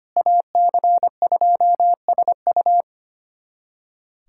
Morse code, heard as a keyed tone, about 25 wpm, 700 Hz. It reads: AC2SU
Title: USA Callsign DE Callsign - 5 Characters - 25wpm